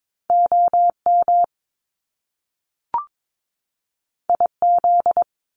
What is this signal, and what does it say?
Morse code, heard as a keyed tone, about 22 wpm, 700 Hz. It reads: OM IZ
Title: Top Two Letter Combinations in Words 22wpm